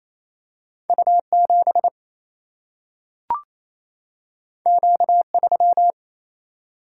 Morse code, heard as a keyed tone, about 28 wpm, 700 Hz. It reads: U 7 Q 3
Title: Single-Letter Single-Number 28wpm